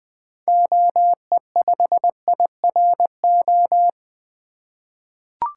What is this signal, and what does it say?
Morse code, heard as a keyed tone, about 20 wpm, 700 Hz. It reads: OE5IRO
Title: 2023 SOTA Activator Call Signs - 20wpm